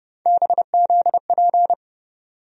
Morse code, heard as a keyed tone, about 32 wpm, 700 Hz. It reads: BZP
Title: USA Call Sign Postfixes - 3 Letters 30wpm